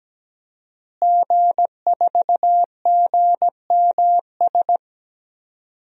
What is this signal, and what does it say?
Morse code, heard as a keyed tone, about 17 wpm, 700 Hz. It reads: G4GMS
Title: DXCC-100 Call Signs - 5 Characters - Repeated 3x - 17wpm